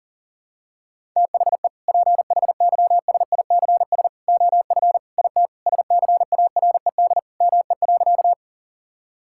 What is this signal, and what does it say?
Morse code, heard as a keyed tone, about 40 wpm, 700 Hz. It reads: THE PHYSICS OF IT SCARED ME.
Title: Sentences from Top 1000 Words - Review Part 5 - 40wpm